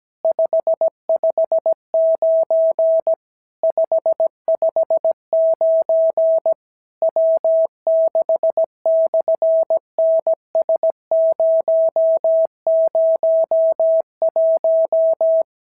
Morse code, heard as a keyed tone, about 17 wpm, 650 Hz. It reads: 559 559 W6/NS001
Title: SOTA QSO Practice 17wpm